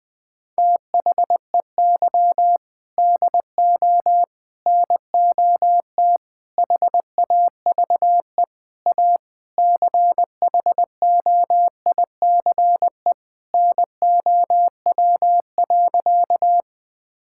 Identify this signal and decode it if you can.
Morse code, heard as a keyed tone, about 20 wpm, 700 Hz. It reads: THEY DO NOT HAVE A CHOICE NOW.
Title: Sentences with 5-Word Phrases - Spoken Beginning - 20wpm